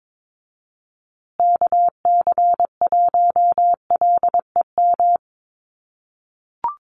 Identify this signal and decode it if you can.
Morse code, heard as a keyed tone, about 22 wpm, 700 Hz. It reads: KC1LEM
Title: USA Call Signs Repeated 3x 22wpm